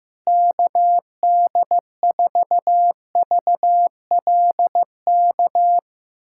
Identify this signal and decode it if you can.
Morse code, heard as a keyed tone, about 15 wpm, 700 Hz. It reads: KD4VLK
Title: USA Callsign DE Callsign 15wpm